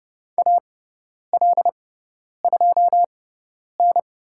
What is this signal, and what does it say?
Morse code, heard as a keyed tone, about 30 wpm, spaced as for 10 wpm, 700 Hz. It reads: AL2N
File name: Lesson 16g - Letter c - Call Signs 30wpm